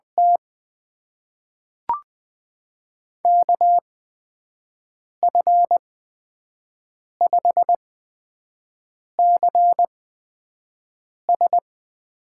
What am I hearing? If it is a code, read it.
Morse code, heard as a keyed tone, about 20 wpm, spaced as for 5 wpm, 700 Hz. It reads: T KF5CS
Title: USA Call Signs - 8x Character Spacing - 20wpm